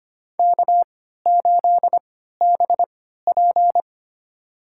Morse code, heard as a keyed tone, about 25 wpm, spaced as for 14 wpm, 700 Hz. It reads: K8BP
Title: USA Call Signs - 4 Characters - 3x Character Spacing - 25wpm